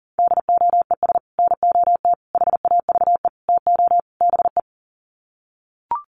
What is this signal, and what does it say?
Morse code, heard as a keyed tone, about 40 wpm, 700 Hz. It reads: DOES NOT HAVE TO BE
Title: Sets of 5 Words 40wpm